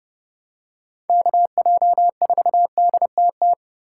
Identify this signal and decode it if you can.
Morse code, heard as a keyed tone, about 28 wpm, 700 Hz. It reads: KJ4DTT